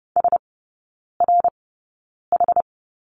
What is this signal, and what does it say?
Morse code, heard as a keyed tone, about 30 wpm, spaced as for 9 wpm, 700 Hz. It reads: SRH
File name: 081 Lesson 14e - Number 2 - Three Characters 30wpm